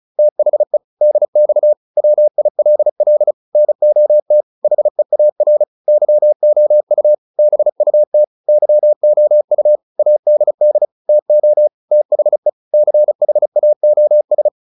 Morse code, heard as a keyed tone, about 34 wpm, 600 Hz. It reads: THE DX WILL NOT HEAR YOU BUT YOU ADD TO THE CHAOS